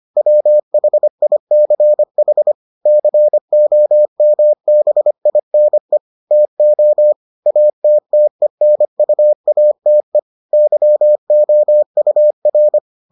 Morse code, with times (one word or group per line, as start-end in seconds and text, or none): 0.2-2.5 WHICH
2.9-6.0 COMBINE
6.3-7.1 TO
7.5-10.2 ATTENUATE
10.5-12.8 YOUR